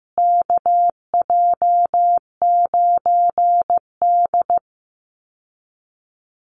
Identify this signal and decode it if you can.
Morse code, heard as a keyed tone, about 15 wpm, 700 Hz. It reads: KJ9D
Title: USA Call Signs Repeated 3x 15wpm